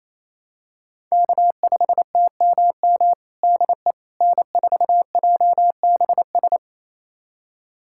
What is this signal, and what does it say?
Morse code, heard as a keyed tone, about 27 wpm, 700 Hz. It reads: K5TMM DE N4JBS
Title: USA Callsign DE Callsign - 5 Characters - 28wpm